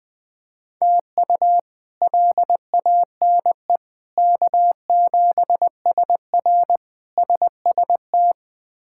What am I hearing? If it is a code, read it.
Morse code, heard as a keyed tone, about 20 wpm, 700 Hz. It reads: TU LANE K7SR SST
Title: K1USN SST QSO Practice - 20wpm